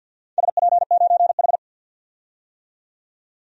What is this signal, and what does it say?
Morse code, heard as a keyed tone, about 46 wpm, 700 Hz. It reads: SP9H
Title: DXCC-100 Call Signs - 4 Characters - 50wpm